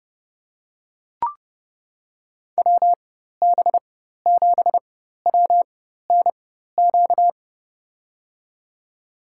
Morse code, heard as a keyed tone, about 30 wpm, spaced as for 14 wpm, 700 Hz. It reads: WB7WNQ
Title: USA Call Signs - 6 Characters - 4x Character Spacing - 30wpm